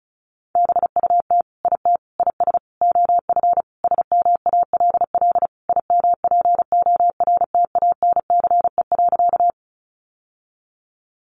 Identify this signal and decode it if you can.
Morse code, heard as a keyed tone, about 34 wpm, 700 Hz. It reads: BUT IT IS OF SMALL IMPORTANCE.
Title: Sentences from Top 1200 Words 35wpm